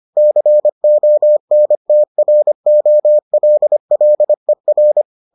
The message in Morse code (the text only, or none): CONTROLLER